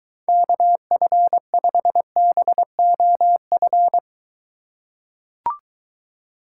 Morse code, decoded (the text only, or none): KF5BOF